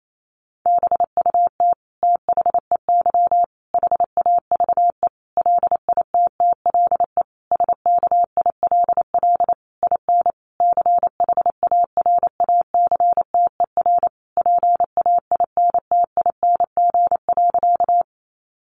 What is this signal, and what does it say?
Morse code, heard as a keyed tone, about 28 wpm, 700 Hz. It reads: BUT THEY HAVE LITTLE SKILL IN CHARACTER PAINTING.
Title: Sentences from Top 1200 Words 28wpm